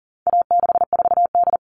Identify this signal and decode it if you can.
Morse code, heard as a keyed tone, about 40 wpm, 700 Hz. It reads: A64D
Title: All Cut Numbers - 4-Digit 40wpm